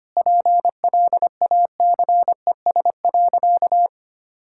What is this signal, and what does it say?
Morse code, heard as a keyed tone, about 25 wpm, 700 Hz. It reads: PLACES.
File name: practice / 25wpm / Sentences with 3-Word Phrases - 25wpm